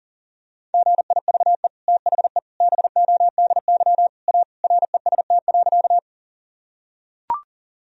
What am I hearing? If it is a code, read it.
Morse code, heard as a keyed tone, about 40 wpm, 700 Hz. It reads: GIVE THE BODY A REST.
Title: Sentences from Top 1000 Words - Review Part 2 - 40wpm